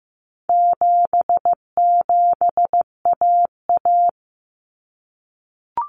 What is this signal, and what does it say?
Morse code, heard as a keyed tone, about 15 wpm, 700 Hz. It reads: 77AA
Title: Common Cut Numbers - 4-Digit 15wpm